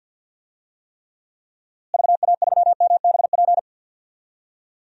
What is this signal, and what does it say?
Morse code, heard as a keyed tone, about 47 wpm, 700 Hz. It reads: VA3MBP
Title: DXCC-100 Call Signs - 6 Characters - Repeated 2x - 50wpm